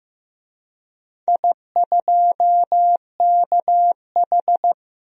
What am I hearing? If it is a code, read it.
Morse code, heard as a keyed tone, about 15 wpm, 700 Hz. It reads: I2KH
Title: DXCC-100 Call Signs - 4 Characters - 15wpm